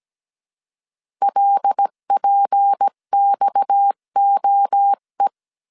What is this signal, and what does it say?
Morse code, heard as a keyed tone, about 17 wpm, 800 Hz. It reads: LPXOE
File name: CW / DET / 1 6 1500C S C 16wpm 2733